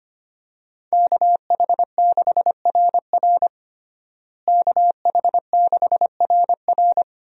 Morse code, heard as a keyed tone, about 25 wpm, 700 Hz. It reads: KH6RR KH6RR
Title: USA Call Signs - 5 Characters - Repeated 2x - 25wpm